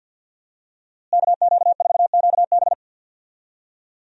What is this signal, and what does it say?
Morse code, heard as a keyed tone, about 51 wpm, 700 Hz. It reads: KQ4QB
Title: USA Call Signs - 5 Characters - Repeated 2x - 50wpm